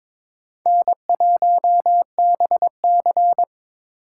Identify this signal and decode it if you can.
Morse code, heard as a keyed tone, about 22 wpm, 700 Hz. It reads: N1BC
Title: USA Call Signs - 4 Characters - 3x Character Spacing - 22wpm